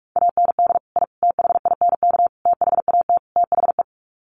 Morse code, heard as a keyed tone, about 44 wpm, 700 Hz. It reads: AND I THINK THAT THE
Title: Sets of 5 Words 45wpm